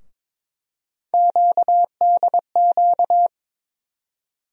Morse code, heard as a keyed tone, about 23 wpm, 700 Hz. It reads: QDQ
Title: USA Call Sign Postfixes - 3 Letters 22wpm